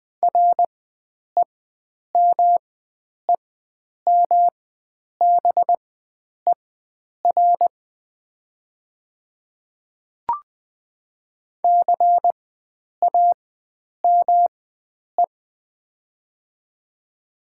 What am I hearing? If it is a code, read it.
Morse code, heard as a keyed tone, about 20 wpm, spaced as for 9 wpm, 700 Hz. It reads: REMEMBER CAME
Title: 4x Character Spacing - Top 500 Words - 20wpm